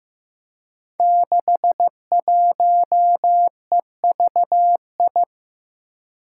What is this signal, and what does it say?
Morse code, heard as a keyed tone, about 15 wpm, 700 Hz. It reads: 61EVI
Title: Five Challenging Letter-Number Combinations 15wpm